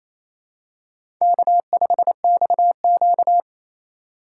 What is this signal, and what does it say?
Morse code, heard as a keyed tone, about 29 wpm, 700 Hz. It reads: K5XQ
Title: K1USN SST QSO Practice - 28wpm